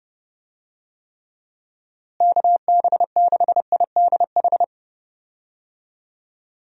Morse code, heard as a keyed tone, about 30 wpm, 700 Hz. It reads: KB6IDH
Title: USA Call Signs - 6 Characters - Repeated 2x - 30wpm